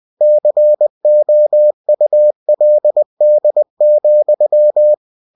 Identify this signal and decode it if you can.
Morse code, heard as a keyed tone, about 20 wpm, 600 Hz. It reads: COULD,
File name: cw-books / winnie-the-pooh / winnie-20wpm-page-0092